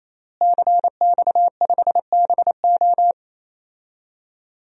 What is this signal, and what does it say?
Morse code, heard as a keyed tone, about 28 wpm, 700 Hz. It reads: CX5BO
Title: DXCC-100 Call Signs - 5 Characters - Repeated 3x - 28wpm